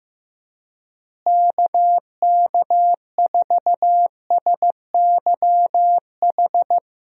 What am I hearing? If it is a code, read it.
Morse code, heard as a keyed tone, about 15 wpm, 700 Hz. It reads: KK4SYH
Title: USA Call Signs - 6 Characters - Repeated 3x - 15wpm